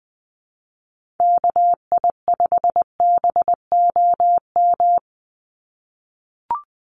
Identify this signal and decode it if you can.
Morse code, heard as a keyed tone, about 20 wpm, 700 Hz. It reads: KI5BOM
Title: USA Call Signs 20wpm